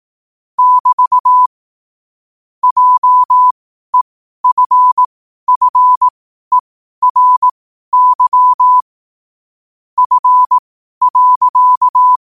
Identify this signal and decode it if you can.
Morse code, heard as a keyed tone, about 18 wpm, spaced as for 12 wpm, 1000 Hz. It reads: <BT> JEFFERY F.
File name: qotdfiles / 12wpm